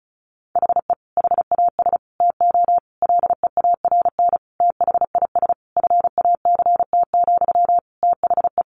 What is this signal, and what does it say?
Morse code, heard as a keyed tone, about 34 wpm, 700 Hz. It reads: HE HAS TO LEARN THIS FACT, THE
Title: Sentences from Top 1200 Words 35wpm